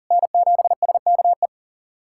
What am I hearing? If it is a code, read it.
Morse code, heard as a keyed tone, about 40 wpm, 700 Hz. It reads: N7SKE
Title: USA Callsign DE Callsign - 5 Characters - 40wpm